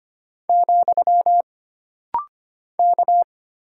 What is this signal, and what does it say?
Morse code, heard as a keyed tone, about 25 wpm, 700 Hz. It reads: , K
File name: Single Character (Letter:Number:Punctuation:Prosign) - Rapid-Fire - 25wpm